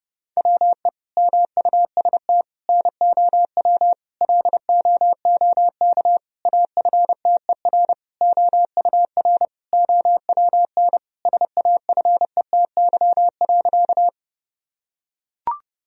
Morse code, heard as a keyed tone, about 32 wpm, 700 Hz. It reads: WE MUST NOW LOOK AFTER OUR OWN SAFETY.
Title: Sentences from Top 1000 Words - Review Part 4 - 30wpm